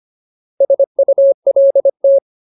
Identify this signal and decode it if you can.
Morse code, heard as a keyed tone, about 25 wpm, 550 Hz. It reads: SULT
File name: Ultimate Word List / 25WPM Ultimate